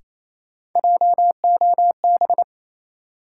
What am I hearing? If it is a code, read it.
Morse code, heard as a keyed tone, about 28 wpm, 700 Hz. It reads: JOB
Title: Sentences from Top 600 Words - New Word Drill - 28wpm